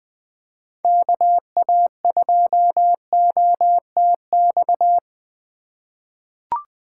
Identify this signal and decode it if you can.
Morse code, heard as a keyed tone, about 20 wpm, 700 Hz. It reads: KA2OTX
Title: USA Call Signs - 6 Characters - 2x Character Spacing - 20wpm